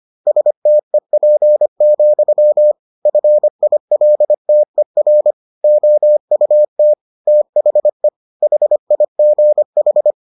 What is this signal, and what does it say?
Morse code, heard as a keyed tone, about 25 wpm, 600 Hz. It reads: STEP, FILTER OUT THE HIGH